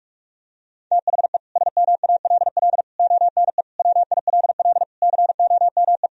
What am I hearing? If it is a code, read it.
Morse code, heard as a keyed tone, about 46 wpm, 700 Hz. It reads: THE SMALL ONE WILL COME
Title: Sentences from Top 1000 Words - Review Part 1 - 45wpm